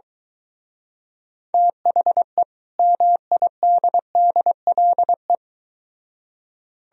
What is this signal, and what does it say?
Morse code, heard as a keyed tone, about 23 wpm, 700 Hz. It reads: THE MIDDLE